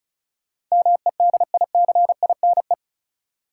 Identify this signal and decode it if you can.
Morse code, heard as a keyed tone, about 34 wpm, 700 Hz. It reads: MEDICINE
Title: Sentences from Top 900 Words - New Word Drill - 35wpm